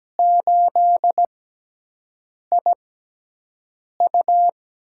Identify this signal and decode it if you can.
Morse code, heard as a keyed tone, about 17 wpm, spaced as for 6 wpm, 700 Hz. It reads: 8IU
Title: USA Call Signs - 5 Characters - 6x Character Spacing - 17wpm